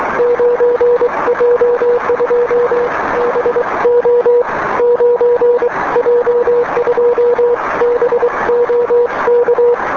Morse code, heard as a keyed tone, about 23 wpm, spaced as for 20 wpm, 480 Hz, with noise. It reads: 9J2BO 9J2BO K
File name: 9J2BO